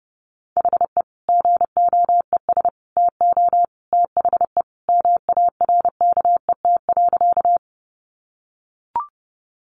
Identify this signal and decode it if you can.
Morse code, heard as a keyed tone, about 30 wpm, 700 Hz. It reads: HE GOES TO THE MARKET.
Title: Additional Easy Phrases 30wpm